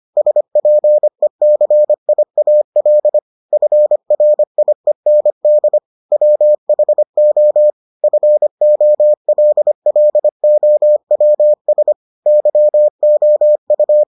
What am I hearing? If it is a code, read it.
Morse code, heard as a keyed tone, about 25 wpm, 600 Hz. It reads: SPECIAL FRIEND WHO FOLLOWS YOU